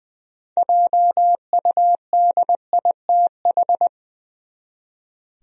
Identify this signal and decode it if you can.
Morse code, heard as a keyed tone, about 20 wpm, 700 Hz. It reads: JUDITH
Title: USA Common Names - 20wpm